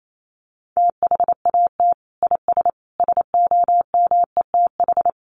Morse code, heard as a keyed tone, about 28 wpm, 700 Hz. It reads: THAT IS SOMETH
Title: Sentences from Top 1100 Words 28wpm